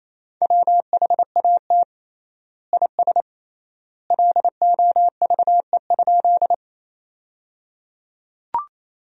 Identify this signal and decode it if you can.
Morse code, heard as a keyed tone, about 28 wpm, 700 Hz. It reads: WHAT IS LOVE?
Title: 3x Word-Spacing - Sentences from Top 400 Words - 28wpm